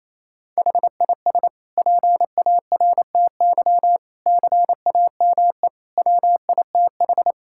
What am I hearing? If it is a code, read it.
Morse code, heard as a keyed tone, about 28 wpm, 700 Hz. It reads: HIS PARTY CAME WITH
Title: Sentences from Top 1600 Words 28wpm